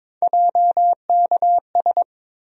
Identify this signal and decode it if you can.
Morse code, heard as a keyed tone, about 22 wpm, 700 Hz. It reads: JKS
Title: USA Call Sign Postfixes - 3 Letters 22wpm